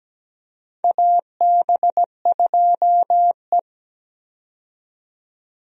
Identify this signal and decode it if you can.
Morse code, heard as a keyed tone, about 17 wpm, 700 Hz. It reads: AB2E
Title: USA Call Signs - 4 Characters - 17wpm